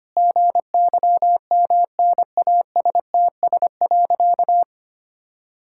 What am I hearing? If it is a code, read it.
Morse code, heard as a keyed tone, about 25 wpm, 700 Hz. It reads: GYMNASTS.
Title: Pangram Challenge v2 - 25wpm